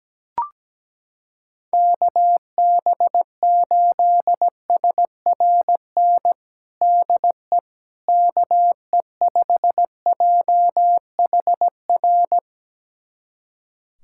Morse code, heard as a keyed tone, about 17 wpm, 700 Hz. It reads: KB8SRN DE KE5JHR
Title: USA Callsign DE Callsign - 6 Characters - 17wpm